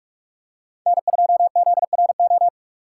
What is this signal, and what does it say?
Morse code, heard as a keyed tone, about 44 wpm, 700 Hz. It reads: N2ZRO